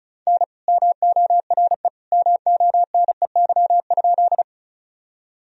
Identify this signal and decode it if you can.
Morse code, heard as a keyed tone, about 36 wpm, 700 Hz. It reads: N MORE MONEY?
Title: Sentences from Top 400 Words and Abbreviations 35wpm